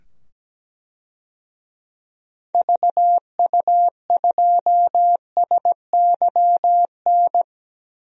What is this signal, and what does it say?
Morse code, heard as a keyed tone, about 17 wpm, 700 Hz. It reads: VU2SYN
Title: DXCC-100 Call Signs - 6 Characters - 17wpm